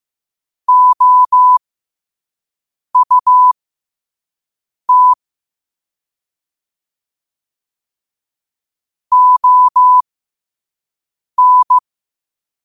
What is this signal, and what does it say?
Morse code, heard as a keyed tone, about 15 wpm, spaced as for 5 wpm, 1000 Hz. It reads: OUT ON
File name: qotdfiles / 5wpm